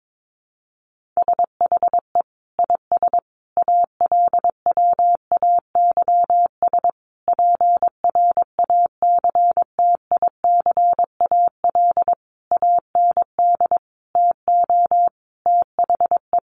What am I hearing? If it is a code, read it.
Morse code, heard as a keyed tone, about 22 wpm, 700 Hz. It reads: SHE IS ALWAYS PRACTICAL AND TO THE
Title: Sentences from Top 1400 Words 22wpm